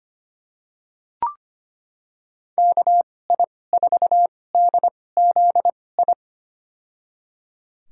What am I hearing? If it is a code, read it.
Morse code, heard as a keyed tone, about 25 wpm, spaced as for 18 wpm, 700 Hz. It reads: KI4DZI
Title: USA Call Signs - 6 Characters - 2x Character Spacing - 25wpm